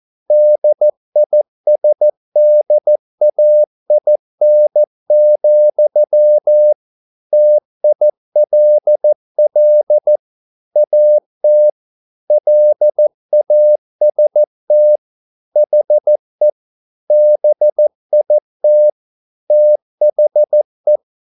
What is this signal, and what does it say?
Morse code, heard as a keyed tone, about 14 wpm, 600 Hz. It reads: DISDAIN, TILL AT LAST HE BIT THE
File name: mp3 files / a dogs mistake - 14wpm